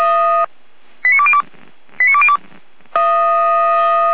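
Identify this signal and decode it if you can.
Five-tone selective call (ZVEI), heard as quick runs of five tones, tones of about 70 ms. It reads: 89202 89202
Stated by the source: ZVEI five-tone selective call received on an SDR, from the Feuerwehr Orlamünde fire brigade audio archive.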